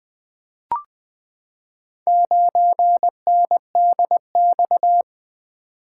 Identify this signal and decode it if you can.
Morse code, heard as a keyed tone, about 20 wpm, 700 Hz. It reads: 9NDX
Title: Four Letter-Number Combinations - 20wpm